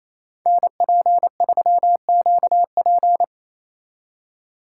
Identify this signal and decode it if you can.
Morse code, heard as a keyed tone, about 34 wpm, 700 Hz. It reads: NP3QP
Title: Speed Racing - USA 2x2 Call Signs 28wpm